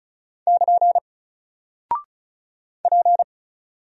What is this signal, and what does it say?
Morse code, heard as a keyed tone, about 36 wpm, 700 Hz. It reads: <KN> P
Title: Single Character (Letter:Number:Punctuation:Prosign) 35wpm